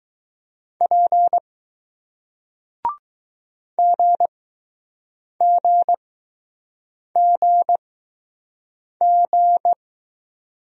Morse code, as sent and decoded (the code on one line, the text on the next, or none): .--. / --. --. --. --.
P GGGG